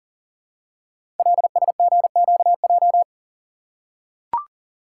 Morse code, heard as a keyed tone, about 40 wpm, 700 Hz. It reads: LSGQJ
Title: Code Groups - 5 Letter:Numbers - 40wpm